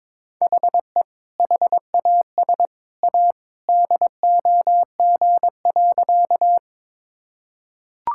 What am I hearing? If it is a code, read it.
Morse code, heard as a keyed tone, about 21 wpm, 700 Hz. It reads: HE HAS A DOG.
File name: practice / 22wpm / Sentences with 2-Word Phrases - 22wpm